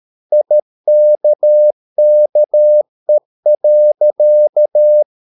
Morse code, heard as a keyed tone, about 13 wpm, 600 Hz. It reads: IKKE.
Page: Ferie 13 wpm